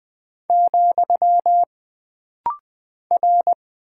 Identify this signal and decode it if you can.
Morse code, heard as a keyed tone, about 20 wpm, 700 Hz. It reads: , R
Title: Single Character (Letter:Number:Punctuation:Prosign) - Rapid-Fire - 20wpm